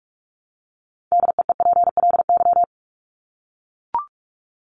Morse code, heard as a keyed tone, about 44 wpm, 700 Hz. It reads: DEEPLY
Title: LY Suffix - Top 100 Words with Base Words 45wpm